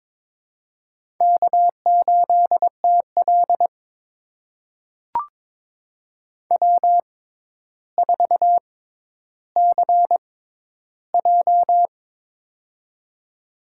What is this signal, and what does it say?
Morse code, heard as a keyed tone, about 23 wpm, spaced as for 11 wpm, 700 Hz. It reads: K8TL W4CJ
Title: USA Call Signs - 4 Characters - 6x Character Spacing - 22wpm